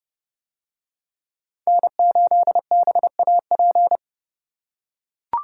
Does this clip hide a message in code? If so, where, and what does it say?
Yes: Morse code from 1.7 to 4.0 s, heard as a keyed tone, about 29 wpm, 700 Hz. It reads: N8BAP